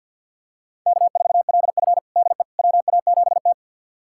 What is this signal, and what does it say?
Morse code, heard as a keyed tone, about 51 wpm, 700 Hz. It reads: K4LF DE WA7T
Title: USA Callsign DE Callsign - 4 Characters - 50wpm